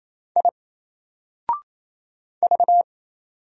Morse code, heard as a keyed tone, about 27 wpm, 700 Hz. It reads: I V
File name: Single-Letter-Number-28wpm